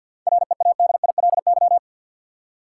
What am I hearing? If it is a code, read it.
Morse code, heard as a keyed tone, about 52 wpm, 700 Hz. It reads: READILY
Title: Seven Letter Words - 50wpm